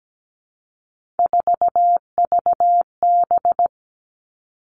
Morse code, heard as a keyed tone, about 17 wpm, 700 Hz. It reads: 4VB
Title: Three Challenging Letter-Number Combinations 17wpm